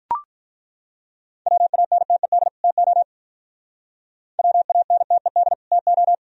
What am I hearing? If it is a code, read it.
Morse code, heard as a keyed tone, about 42 wpm, 700 Hz. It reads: WANTED TO WANTED TO
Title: Speed-Racing - Sets of 2 Words 35wpm